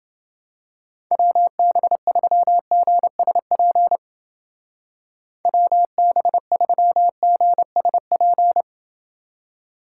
Morse code, heard as a keyed tone, about 28 wpm, 700 Hz. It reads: WB3GSP WB3GSP